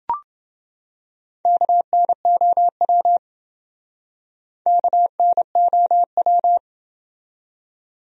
Morse code, heard as a keyed tone, about 27 wpm, 700 Hz. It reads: KNOW KNOW